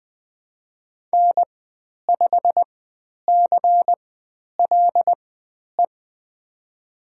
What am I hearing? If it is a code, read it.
Morse code, heard as a keyed tone, about 20 wpm, spaced as for 10 wpm, 700 Hz. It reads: N5CLE
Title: 102 Lesson 17g - Letter u - Call Signs 20wpm